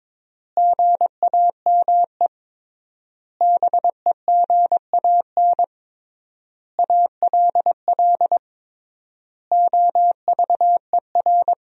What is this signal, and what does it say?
Morse code, heard as a keyed tone, about 21 wpm, 700 Hz. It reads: GAME BEGAN ALL OVER
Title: 3x Word-Spacing - Sentences from Top 500 Words - 22wpm